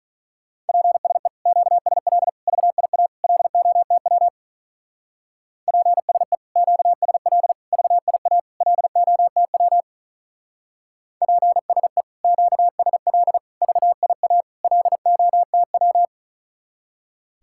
Morse code, heard as a keyed tone, about 42 wpm, 700 Hz. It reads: PSE QSL VIA LOTW PSE QSL VIA LOTW PSE QSL VIA LOTW
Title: Speed-Racing Common QSO Phrases - 35wpm